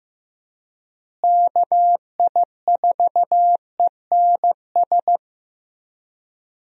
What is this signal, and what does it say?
Morse code, heard as a keyed tone, about 15 wpm, 700 Hz. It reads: KI4ENS
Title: USA Call Signs - 6 Characters - Repeated 3x - 15wpm